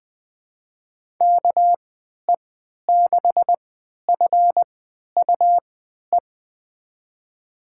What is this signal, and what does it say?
Morse code, heard as a keyed tone, about 20 wpm, spaced as for 11 wpm, 700 Hz. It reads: KE6FUE
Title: USA Call Signs - 3x Character Spacing - 20wpm